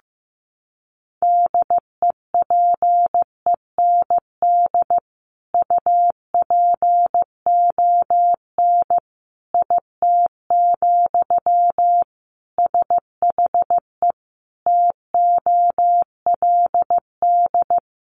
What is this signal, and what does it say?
Morse code, heard as a keyed tone, about 15 wpm, 700 Hz. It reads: DEPEND UPON IT, SHE TOLD
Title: Sentences from Top 1300 Words 15wpm